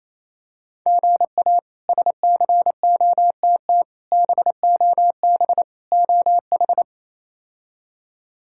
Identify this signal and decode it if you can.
Morse code, heard as a keyed tone, about 28 wpm, 700 Hz. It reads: GA SCOTT BOB OH
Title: K1USN SST QSO Practice - 28wpm